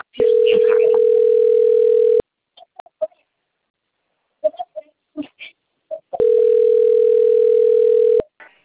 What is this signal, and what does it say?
Ringback tone (North America), 440 + 480 Hz.
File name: out- - - - - .80038